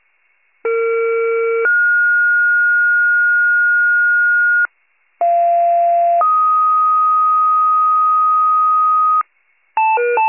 Two-tone page, heard as a one-second tone, then a three-second tone, about 460 then 1430 Hz, about 690 then 1250 Hz, with noise. It is heard from 0.6 to 9.2 s.
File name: 41 FIRE